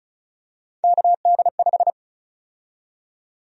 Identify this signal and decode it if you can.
Morse code, heard as a keyed tone, about 36 wpm, 700 Hz. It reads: KD5
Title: USA Call Sign Prefixes - Repeated 2x - 35wpm